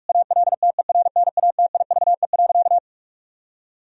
Morse code, heard as a keyed tone, about 47 wpm, 700 Hz. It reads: ALTERNATIVE.
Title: Sentences from Top 1000 Words - Review Part 4 - 45wpm